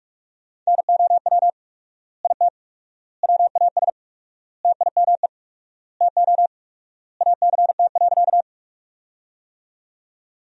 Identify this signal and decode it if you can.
Morse code, heard as a keyed tone, about 47 wpm, 700 Hz. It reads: NOW IT WAS TIME TO ACT.
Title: 4x Word-Spacing - Sentences from Top 200 Words - 45wpm